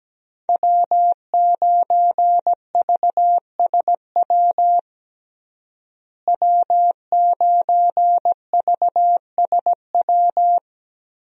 Morse code, heard as a keyed tone, about 17 wpm, 700 Hz. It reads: W9VSW W9VSW
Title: USA Call Signs - 5 Characters - Repeated 3x - 17wpm